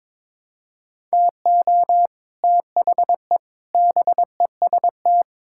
Morse code, heard as a keyed tone, about 22 wpm, 700 Hz. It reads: TO THE BEST
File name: practice / 22wpm / Sets of 8 Words - 22wpm